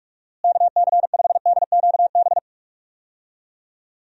Morse code, heard as a keyed tone, about 46 wpm, 700 Hz. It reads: KC5DQB
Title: USA Call Signs - 6 Characters - Repeated 3x - 45wpm